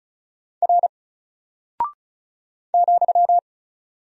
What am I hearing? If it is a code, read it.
Morse code, heard as a keyed tone, about 36 wpm, 700 Hz. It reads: R ,
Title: Single Character (Letter:Number:Punctuation:Prosign) Alt 35wpm